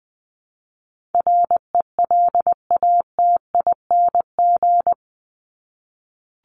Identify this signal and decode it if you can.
Morse code, heard as a keyed tone, about 20 wpm, 700 Hz. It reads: RELATING
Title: ING Suffix - Top 100 Words with Base Words 20wpm